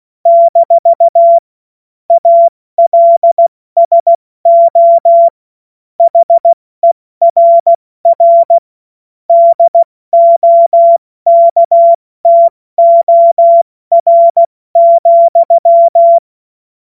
Morse code, heard as a keyed tone, about 16 wpm, 680 Hz. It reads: - ALSO HERR DOKTOR,
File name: Morsekurs / Witze / Witz 11. Juli 2023 70 F80